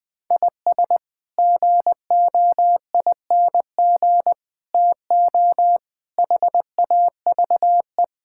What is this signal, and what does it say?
Morse code, heard as a keyed tone, about 20 wpm, 700 Hz. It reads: IS GOING TO HAVE
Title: Build A Phrase - 5-Word Phrases - 20wpm